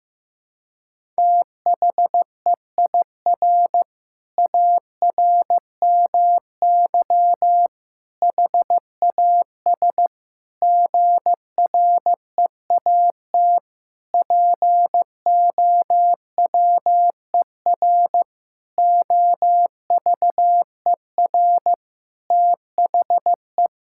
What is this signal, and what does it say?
Morse code, heard as a keyed tone, about 15 wpm, 700 Hz. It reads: THEIR ARMY HAS GREAT POWER OVER THE
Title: Sentences from Top 1000 Words - Review Part 5 - 15wpm